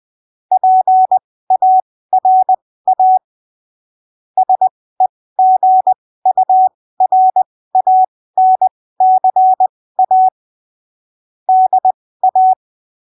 Morse code, heard as a keyed tone, about 20 wpm, spaced as for 15 wpm, 750 Hz. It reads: PARA SEGURANCA DA
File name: arquivos / texto / texto 03 20ppm